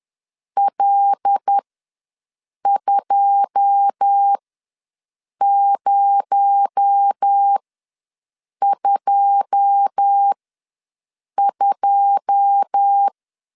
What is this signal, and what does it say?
Morse code, heard as a keyed tone, about 10 wpm, spaced as for 6 wpm, 800 Hz. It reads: L2022
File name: LEZIONI / 9 2 ws 10wpm 4130